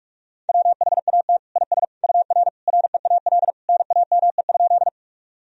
Morse code, heard as a keyed tone, about 46 wpm, 700 Hz. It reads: WHAT IS UR REAL NAME?
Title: Sentences from Top 300 Words and Abbreviations 45wpm